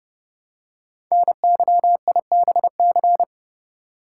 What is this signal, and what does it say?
Morse code, heard as a keyed tone, about 30 wpm, 700 Hz. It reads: NYIBC
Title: Code Groups - 5 Letter:Numbers - 30wpm